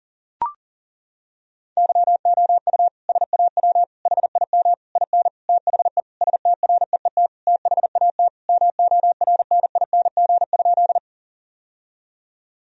Morse code, heard as a keyed tone, about 37 wpm, 700 Hz. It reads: YOU SAW HIM IN THE STREET THAT MORNING?
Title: Sentences from Top 1000 Words - Review Part 3 - 40wpm